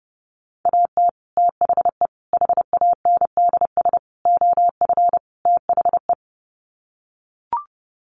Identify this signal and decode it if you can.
Morse code, heard as a keyed tone, about 30 wpm, 700 Hz. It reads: AT THE HANDS OF THE
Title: Sets of 5 Words Encore2 30wpm